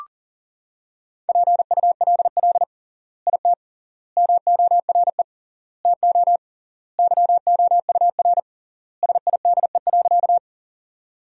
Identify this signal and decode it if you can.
Morse code, heard as a keyed tone, about 37 wpm, 700 Hz. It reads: PULL IT MORE TO YOUR SIDE.
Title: 3x Word-Spacing - Sentences from Top 400 Words - 40wpm